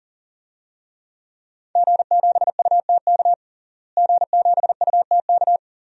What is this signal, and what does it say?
Morse code, heard as a keyed tone, about 41 wpm, 700 Hz. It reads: G7UTK G7UTK
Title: DXCC-100 Call Signs - 5 Characters - Repeated 2x - 40wpm